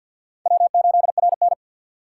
Morse code, heard as a keyed tone, about 47 wpm, 700 Hz. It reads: W8RN
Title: USA Call Signs - 4 Characters - 2x Character Spacing - 50wpm